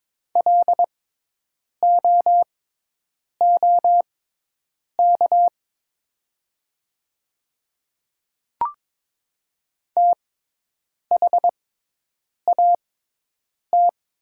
Morse code, heard as a keyed tone, about 22 wpm, spaced as for 8 wpm, 700 Hz. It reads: LOOK THAT
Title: 6x Character Spacing - Top 100 Words - 22wpm